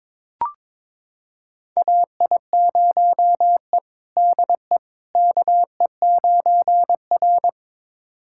Morse code, heard as a keyed tone, about 22 wpm, 700 Hz. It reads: AI0E DE KE9R
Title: USA Callsign DE Callsign - 4 Characters - 22wpm